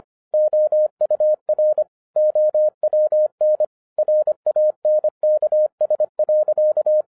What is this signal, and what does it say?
Morse code, heard as a keyed tone, about 26 wpm, 610 Hz. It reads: OUR OWN RANKS.
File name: 2012-01-mp3 / n14-FUN WITH NUMBERS-25wpm